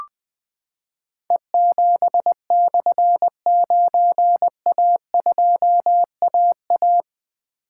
Morse code, heard as a keyed tone, about 20 wpm, 700 Hz. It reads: E7/9A2AA
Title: Difficult Call Signs 20wpm